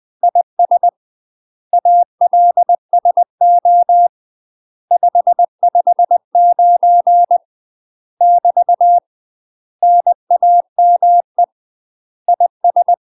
Morse code, heard as a keyed tone, about 20 wpm, 700 Hz. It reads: IS ALSO 559 <BT> NAME IS
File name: UK MASTER QSOs TRAINING / 20WPM / UK MASTER QSOs 0065